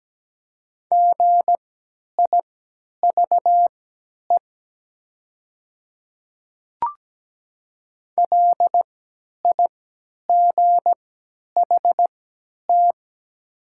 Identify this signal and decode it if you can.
Morse code, heard as a keyed tone, about 17 wpm, spaced as for 10 wpm, 690 Hz. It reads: GIVE LIGHT
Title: 3x Character Spacing - Top 300 Words - 17wpm